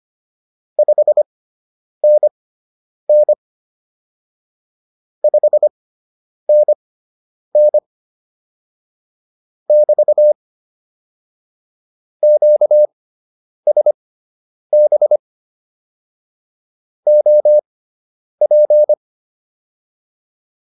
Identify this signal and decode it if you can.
Morse code, heard as a keyed tone, about 25 wpm, spaced as for 9 wpm, 600 Hz. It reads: 5NN 5NN <BT> QSB OP